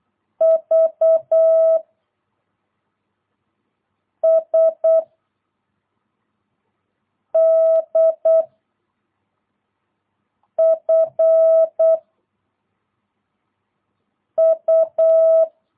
Morse code, heard as a keyed tone, about 8 wpm, 660 Hz. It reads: V S D F U
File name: jh8cbh / 2022cw